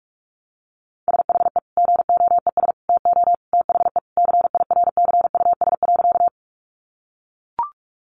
Morse code, heard as a keyed tone, about 44 wpm, 700 Hz. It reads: SHE GOES TO THE CIRCUS.
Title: Easy Phrases 45wpm